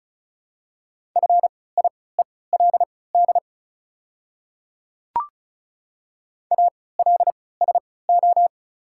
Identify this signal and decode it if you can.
Morse code, heard as a keyed tone, about 34 wpm, spaced as for 20 wpm, 700 Hz. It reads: FIELD ALSO
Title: 3x Character Spacing - Top 500 Words - 35wpm